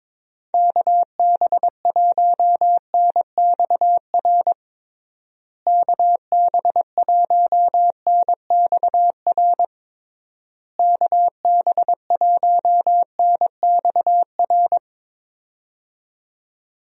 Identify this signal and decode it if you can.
Morse code, heard as a keyed tone, about 22 wpm, 700 Hz. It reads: KB1NXR KB1NXR KB1NXR
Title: USA Call Signs - 6 Characters - Repeated 3x - 22wpm